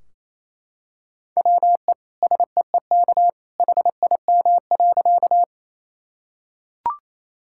Morse code, heard as a keyed tone, about 28 wpm, 700 Hz. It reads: WE SEEK HIM.
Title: Sentences from Top 1600 Words 28wpm